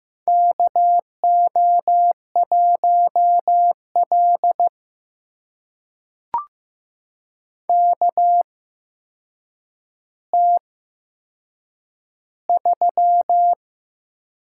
Morse code, heard as a keyed tone, about 15 wpm, spaced as for 7 wpm, 700 Hz. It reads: KO1L KT3
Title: USA Call Signs - 4 Characters - 8x Character Spacing - 15wpm